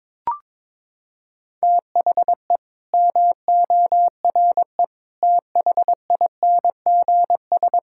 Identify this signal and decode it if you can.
Morse code, heard as a keyed tone, about 22 wpm, 700 Hz. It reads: THE MORE THINGS
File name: Sets of 9 Words - 22wpm